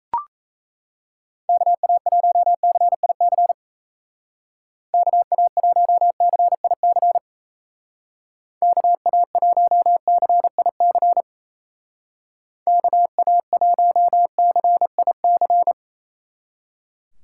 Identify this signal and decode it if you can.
Morse code, heard as a keyed tone, about 34 wpm, 700 Hz. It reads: KA1CIC KA1CIC KA1CIC KA1CIC
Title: Speed Racing - USA 2x3 Call Signs 28wpm